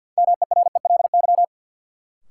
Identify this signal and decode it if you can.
Morse code, heard as a keyed tone, about 53 wpm, 700 Hz. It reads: MERELY
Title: Sentences from Top 1400 Words - New Word Drill Alt - 50wpm